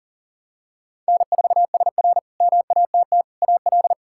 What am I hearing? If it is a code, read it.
Morse code, heard as a keyed tone, about 40 wpm, 700 Hz. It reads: N4SR MATT AL
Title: K1USN SST QSO Practice - 40wpm